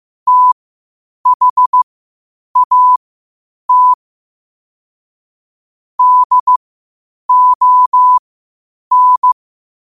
Morse code, heard as a keyed tone, about 15 wpm, spaced as for 8 wpm, 1000 Hz. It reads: THAT DON
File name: qotdfiles / 8wpm